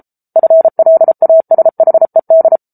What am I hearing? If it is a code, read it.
Morse code, heard as a keyed tone, about 34 wpm, 650 Hz. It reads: FLASHED